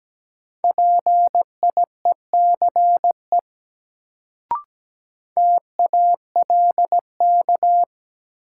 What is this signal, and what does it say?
Morse code, heard as a keyed tone, about 17 wpm, 700 Hz. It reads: PIECE TALK
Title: Sentences from Top 400 Words - New Word Drill - 17wpm